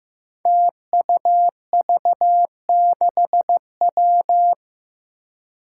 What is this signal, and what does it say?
Morse code, heard as a keyed tone, about 15 wpm, 700 Hz. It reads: TUV6W
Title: Five Challenging Letter-Number Combinations 15wpm